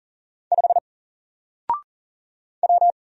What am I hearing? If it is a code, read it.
Morse code, heard as a keyed tone, about 40 wpm, 700 Hz. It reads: 5 W
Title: Single Character (Letter:Number:Punctuation:Prosign) Alt 40wpm